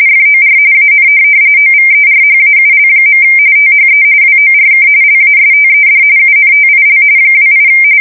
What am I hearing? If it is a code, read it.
RTTY, about 45.45 baud. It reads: FOR AN OVERHAUL. THE WORK LASTED UNTIL 28
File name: fsk170 / USS-Massachusetts